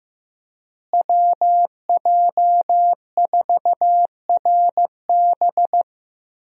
Morse code, heard as a keyed tone, about 15 wpm, 700 Hz. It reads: WJ4RB
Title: USA Call Signs - 5 Characters - Repeated 3x - 15wpm